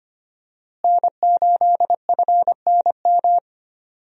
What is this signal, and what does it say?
Morse code, heard as a keyed tone, about 25 wpm, 700 Hz. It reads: N8FNM
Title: USA Call Signs - 5 Characters - Repeated 3x - 25wpm